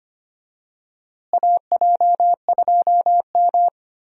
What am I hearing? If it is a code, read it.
Morse code, heard as a keyed tone, about 25 wpm, 700 Hz. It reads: AJ2M
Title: USA Call Signs - 4 Characters - Repeated 3x - 25wpm